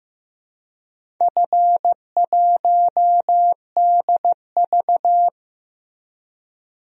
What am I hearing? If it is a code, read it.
Morse code, heard as a keyed tone, about 15 wpm, 700 Hz. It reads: F1DV
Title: DXCC-100 Call Signs - 4 Characters - Repeated 2x - 15wpm